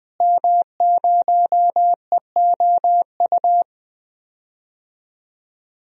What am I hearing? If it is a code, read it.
Morse code, heard as a keyed tone, about 20 wpm, 700 Hz. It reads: M0EOU
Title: DXCC-100 Call Signs - 5 Characters - Repeated 3x - 20wpm